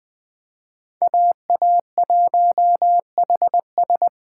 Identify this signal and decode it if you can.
Morse code, heard as a keyed tone, about 20 wpm, 700 Hz. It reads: AA1HS